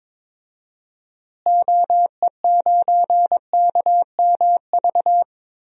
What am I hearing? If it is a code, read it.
Morse code, heard as a keyed tone, about 22 wpm, 700 Hz. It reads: OE9KMV
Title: DXCC-100 Call Signs - 6 Characters - Repeated 3x - 22wpm